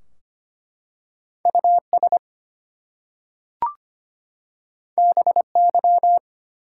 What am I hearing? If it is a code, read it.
Morse code, heard as a keyed tone, about 25 wpm, 700 Hz. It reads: US BY
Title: Two Letter Words - 25wpm